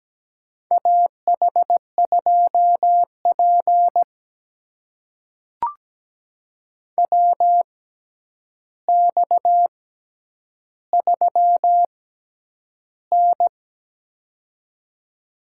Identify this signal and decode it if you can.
Morse code, heard as a keyed tone, about 17 wpm, spaced as for 9 wpm, 700 Hz. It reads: AH2P WX3N
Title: USA Call Signs - 6x Character Spacing - 17wpm